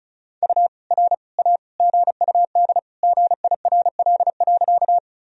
Morse code, heard as a keyed tone, about 34 wpm, 700 Hz. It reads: U R A GUD GIRL.
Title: Sentences from Top 400 Words and Abbreviations 35wpm